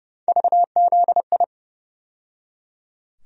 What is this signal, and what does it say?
Morse code, heard as a keyed tone, about 31 wpm, 700 Hz. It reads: VZI
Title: USA Call Sign Postfixes - 3 Letters 30wpm